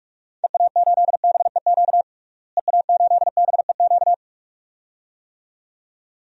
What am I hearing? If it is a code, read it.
Morse code, heard as a keyed tone, about 47 wpm, 700 Hz. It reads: EA8BEQ EA8BEQ
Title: DXCC-100 Call Signs - 6 Characters - Repeated 2x - 45wpm